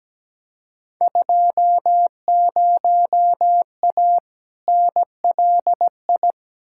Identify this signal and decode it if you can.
Morse code, heard as a keyed tone, about 17 wpm, 700 Hz. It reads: 20A NLI
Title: Field Day Exchanges Encore - 17wpm